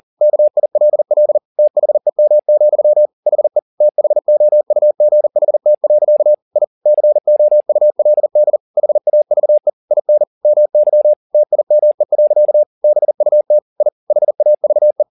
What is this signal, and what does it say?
Morse code, heard as a keyed tone, about 40 wpm, 600 Hz. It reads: KILL THEM, HE THOUGHT. I COULD HAVE IN MY TIME. BUT I HAVE
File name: the old man and the sea 2 40wpm